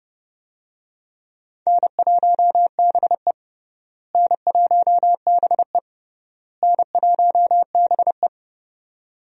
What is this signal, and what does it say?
Morse code, heard as a keyed tone, about 30 wpm, 700 Hz. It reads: N1BE N1BE N1BE
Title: USA Call Signs - 4 Characters - Repeated 3x - 30wpm